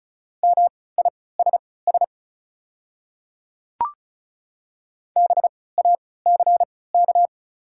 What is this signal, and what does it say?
Morse code, heard as a keyed tone, about 36 wpm, spaced as for 20 wpm, 700 Hz. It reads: MISS BACK
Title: 3x Character Spacing - Top 500 Words - 35wpm